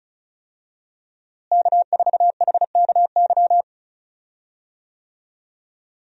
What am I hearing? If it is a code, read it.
Morse code, heard as a keyed tone, about 36 wpm, 700 Hz. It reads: K4HKY
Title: USA Call Signs - 5 Characters - Repeated 2x - 35wpm